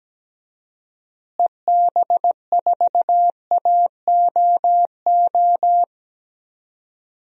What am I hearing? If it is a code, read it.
Morse code, heard as a keyed tone, about 17 wpm, 700 Hz. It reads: EB4AOO
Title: DXCC-100 Call Signs - 6 Characters - Repeated 3x - 17wpm